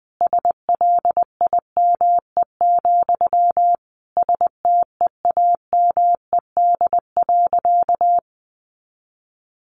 Morse code, heard as a keyed tone, about 20 wpm, 700 Hz. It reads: SLIME, STEAMED.
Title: A Sound Of Thunder 20wpm